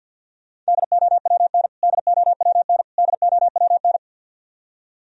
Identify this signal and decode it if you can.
Morse code, heard as a keyed tone, about 54 wpm, 700 Hz. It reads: DOWN DOWN DOWN
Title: Top 100 Words Repeated 50wpm